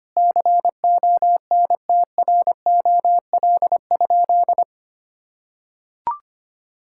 Morse code, heard as a keyed tone, about 25 wpm, 700 Hz. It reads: CONTROL?
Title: Sentences from Top 1000 Words - Review Part 5 - 25wpm